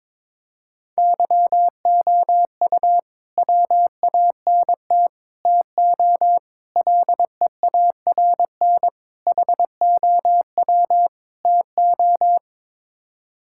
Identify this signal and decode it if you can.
Morse code, heard as a keyed tone, about 22 wpm, 700 Hz. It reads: YOU WANT TO LEARN HOW TO
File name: practice / 22wpm / Sets of 6 Words - 22wpm